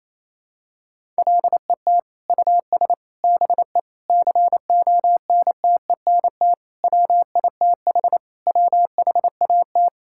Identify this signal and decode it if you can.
Morse code, heard as a keyed tone, about 28 wpm, 700 Hz. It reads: LET US BE CONTENT WITH WHAT
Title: Sentences from Top 1000 Words - Review Part 6 - 28wpm